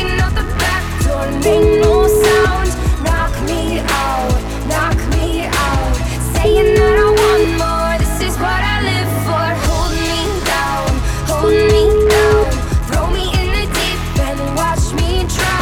Ringback tone (Europe, 425 Hz), with noise.